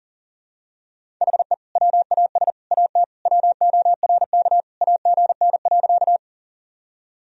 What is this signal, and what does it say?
Morse code, heard as a keyed tone, about 40 wpm, 700 Hz. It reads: HE WAS AT WORK AGN.
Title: Sentences from Top 200 Words and Abbreviations 40wpm